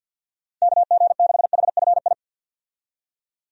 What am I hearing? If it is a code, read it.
Morse code, heard as a keyed tone, about 51 wpm, 700 Hz. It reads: KG6HFI
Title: USA Call Signs - 6 Characters - 50wpm